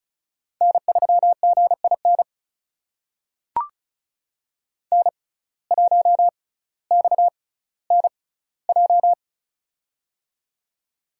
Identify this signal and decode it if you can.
Morse code, heard as a keyed tone, about 36 wpm, spaced as for 18 wpm, 700 Hz. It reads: N3GIN N1XNJ
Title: USA Call Signs - 5 Characters - 6x Character Spacing - 35wpm